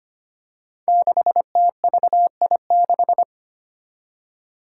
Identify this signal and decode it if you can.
Morse code, heard as a keyed tone, about 25 wpm, 700 Hz. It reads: 6TVI6
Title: Five Challenging Letter-Number Combinations 25wpm